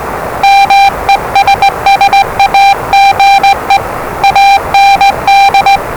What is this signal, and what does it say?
Morse code, heard as a keyed tone, about 20 wpm, 790 Hz, with noise. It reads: MESSAGE AND
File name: WellingtonRadioZlwCloseDownOnHfAndMf / 01-Zlw 4277khz 417.5 qrt until 0925z